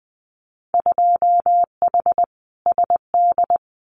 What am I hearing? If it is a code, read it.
Morse code, heard as a keyed tone, about 20 wpm, 700 Hz. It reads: 2H SD
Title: Winter Field Day Exchanges 20wpm